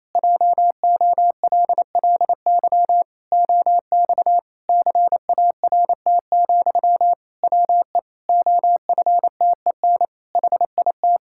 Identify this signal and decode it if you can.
Morse code, heard as a keyed tone, about 28 wpm, 700 Hz. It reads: JOLLY OX CART, WE OFTEN HIT
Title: Pangram Challenge v2 - 28wpm